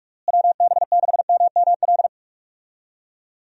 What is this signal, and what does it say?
Morse code, heard as a keyed tone, about 46 wpm, 700 Hz. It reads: WB6MML
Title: USA Call Signs - 6 Characters - Repeated 3x - 45wpm